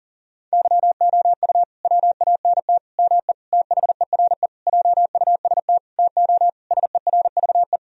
Morse code, heard as a keyed tone, about 40 wpm, 700 Hz. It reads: YOU WANT ME THERE JUST TO SERVE
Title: Sentences from Top 1000 Words - Review Part 2 - 40wpm